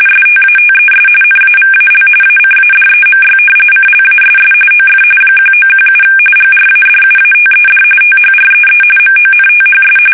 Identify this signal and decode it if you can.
RTTY, about 45.45 baud. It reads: STATES NAVAL ACADEMY CLASS OF 1917 GRADUATED THREE MONTHS
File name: fsk850